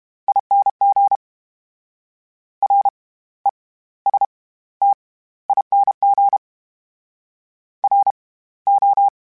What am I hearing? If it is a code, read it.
Morse code, heard as a keyed tone, about 32 wpm, spaced as for 17 wpm, 800 Hz. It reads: ING RESTING RO
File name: Special INGs / Special INGs-mp3s / SlowToFaster ing @30